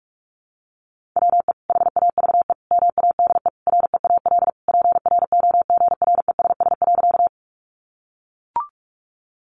Morse code, heard as a keyed tone, about 44 wpm, 700 Hz. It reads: WE HAVE MADE REAL PROGRESS.
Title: Sentences from Top 1400 Words 45wpm